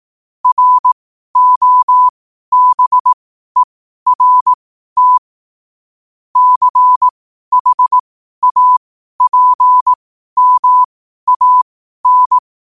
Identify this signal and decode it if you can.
Morse code, heard as a keyed tone, about 18 wpm, spaced as for 12 wpm, 1000 Hz. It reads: ROBERT CHAPMAN
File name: qotdfiles / 12wpm